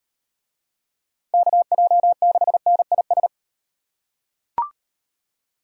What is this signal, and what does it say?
Morse code, heard as a keyed tone, about 36 wpm, 700 Hz. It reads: KJ6NIS